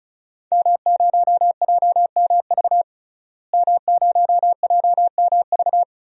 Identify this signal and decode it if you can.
Morse code, heard as a keyed tone, about 34 wpm, 700 Hz. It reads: M0JMV M0JMV
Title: DXCC-100 Call Signs - 5 Characters - Repeated 3x - 35wpm